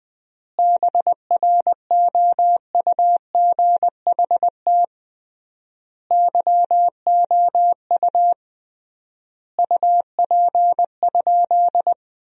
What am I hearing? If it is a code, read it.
Morse code, heard as a keyed tone, about 20 wpm, 700 Hz. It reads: BROUGHT YOU UP?
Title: 3x Word-Spacing - Sentences from Top 500 Words - 20wpm